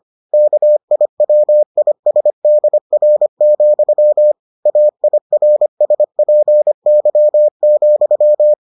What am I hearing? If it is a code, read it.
Morse code, heard as a keyed tone, about 25 wpm, 600 Hz. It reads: KIWISDR, AIRSPY,